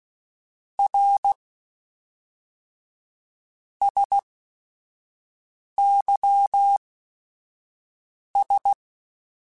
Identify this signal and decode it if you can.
Morse code, heard as a keyed tone, about 16 wpm, spaced as for 5 wpm, 800 Hz. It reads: R SYS
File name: Technical Subjects / Technical Subjects-mp3s / Sw Pwr Sup @5